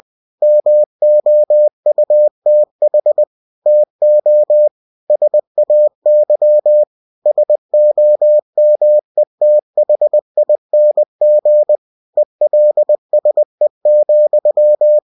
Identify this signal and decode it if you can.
Morse code, heard as a keyed tone, about 20 wpm, 600 Hz. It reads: MOUTH TO SAY SOMETHING ELSE,